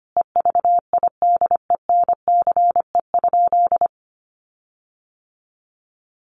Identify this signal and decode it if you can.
Morse code, heard as a keyed tone, about 25 wpm, 700 Hz. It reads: EVIDENCE?
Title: Sentences from Top 1300 Words 25wpm